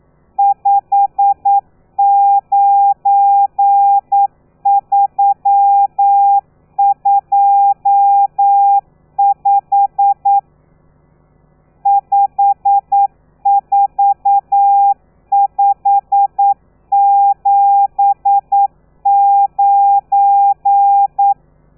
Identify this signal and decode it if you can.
Morse code, heard as a keyed tone, about 9 wpm, 800 Hz. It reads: 59325 54579
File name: telegrafia / i2tfj / Lez12 40